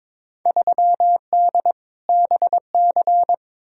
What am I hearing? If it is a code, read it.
Morse code, heard as a keyed tone, about 22 wpm, 700 Hz. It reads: 3D BC
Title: Field Day Exchanges Encore - 22wpm